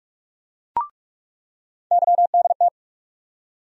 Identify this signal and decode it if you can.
Morse code, heard as a keyed tone, about 44 wpm, 700 Hz. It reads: YDT
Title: USA Call Sign Postfixes - 3 Letters 45wpm